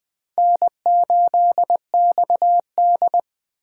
Morse code, heard as a keyed tone, about 20 wpm, 700 Hz. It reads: N8XD
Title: Speed Racing - USA Call Signs 17wpm